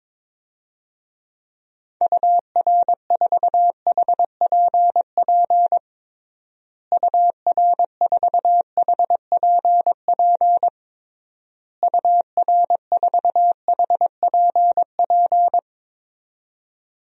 Morse code, heard as a keyed tone, about 21 wpm, 700 Hz. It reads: UR4HPP UR4HPP UR4HPP
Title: DXCC-100 Call Signs - 6 Characters - Repeated 3x - 22wpm